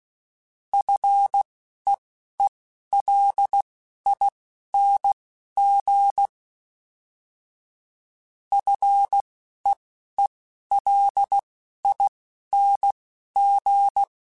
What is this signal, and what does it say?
Morse code, heard as a keyed tone, about 16 wpm, spaced as for 11 wpm, 800 Hz. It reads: FEELING FEELING